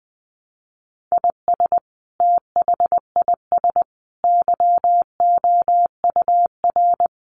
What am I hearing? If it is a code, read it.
Morse code, heard as a keyed tone, about 20 wpm, 700 Hz. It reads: IS THIS YOUR
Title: Sentences from Top 1500 Words 20wpm